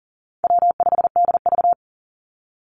Morse code, heard as a keyed tone, about 40 wpm, 700 Hz. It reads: W5DV
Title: Call Signs - 40wpm